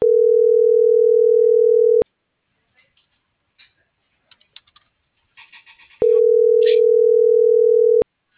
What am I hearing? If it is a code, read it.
Ringback tone (North America), 440 + 480 Hz.